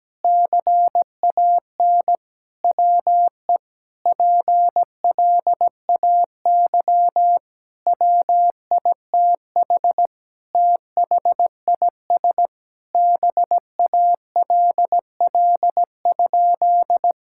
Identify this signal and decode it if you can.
Morse code, heard as a keyed tone, about 17 wpm, 700 Hz. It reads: CAN WE PLAY WITH THIS BALL?
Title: Sentences from Top 1000 Words - Review Part 6 - 17wpm